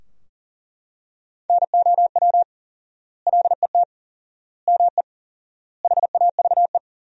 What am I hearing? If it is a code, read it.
Morse code, heard as a keyed tone, about 37 wpm, 700 Hz. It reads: NOW LET ME HAVE
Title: 4x Word-Spacing - Sentences from Top 300 Words - 40wpm